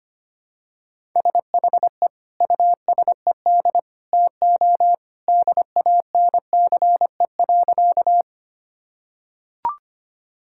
Morse code, heard as a keyed tone, about 25 wpm, 700 Hz. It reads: SHE USED TO DANCE.
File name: practice / 25wpm / Sentences with 2-Word Phrases - 25wpm